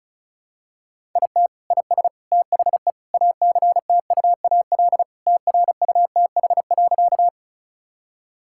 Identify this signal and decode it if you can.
Morse code, heard as a keyed tone, about 36 wpm, 700 Hz. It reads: IT IS THE ACTUAL TRUTH.
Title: Sentences from Top 1000 Words - Review Part 5 - 35wpm